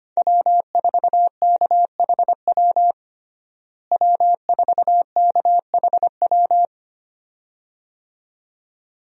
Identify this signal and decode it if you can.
Morse code, heard as a keyed tone, about 25 wpm, 700 Hz. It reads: W4KHW W4KHW
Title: USA Call Signs - 5 Characters - Repeated 2x - 25wpm